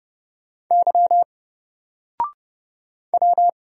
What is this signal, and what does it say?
Morse code, heard as a keyed tone, about 30 wpm, 700 Hz. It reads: Y W
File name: Single-Letter-Number-30wpm